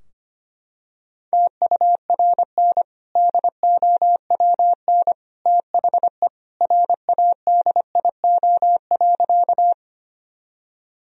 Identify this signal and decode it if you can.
Morse code, heard as a keyed tone, about 25 wpm, 700 Hz. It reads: TURN DOWN THE RADIO.
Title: Sentences from Top 1000 Words - Review Part 3 - 25wpm